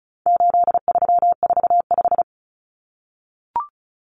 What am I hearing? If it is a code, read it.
Morse code, heard as a keyed tone, about 34 wpm, 700 Hz. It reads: 8345
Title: Common Cut Numbers - 4-Digit 35wpm